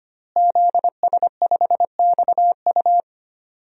Morse code, heard as a keyed tone, about 25 wpm, 700 Hz. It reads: ZS5XU
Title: DXCC-100 Call Signs - 5 Characters - Repeated 3x - 25wpm